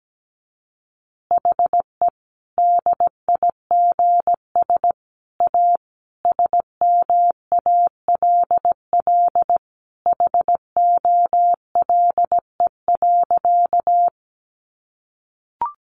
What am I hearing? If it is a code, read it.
Morse code, heard as a keyed tone, about 17 wpm, 700 Hz. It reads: HE DIGS A SMALL HOLE.
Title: Easy Phrases 17wpm